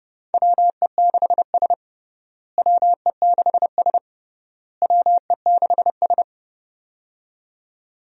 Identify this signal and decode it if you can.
Morse code, heard as a keyed tone, about 30 wpm, 700 Hz. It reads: WE6S WE6S WE6S
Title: USA Call Signs - 4 Characters - Repeated 3x - 30wpm